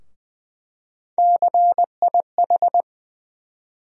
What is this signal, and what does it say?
Morse code, heard as a keyed tone, about 21 wpm, 700 Hz. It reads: CIH
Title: USA Call Sign Postfixes - 3 Letters 20wpm